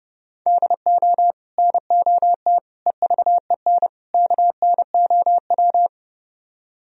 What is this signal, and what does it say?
Morse code, heard as a keyed tone, about 29 wpm, 700 Hz. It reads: DO NOT EVEN KNOW